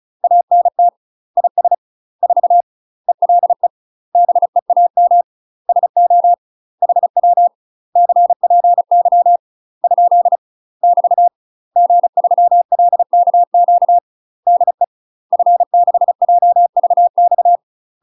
Morse code, heard as a keyed tone, about 34 wpm, 700 Hz. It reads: ANT IS 4 ELE BEAM SO HW CPY ? <BT> G3LKQ DE F6JVX
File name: UK MASTER QSOs TRAINING / 35WPM / UK MASTER QSOs 0021